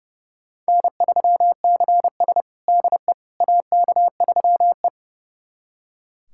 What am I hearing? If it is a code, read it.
Morse code, heard as a keyed tone, about 30 wpm, 700 Hz. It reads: N3CS DE AK3E
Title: USA Callsign DE Callsign - 4 Characters - 30wpm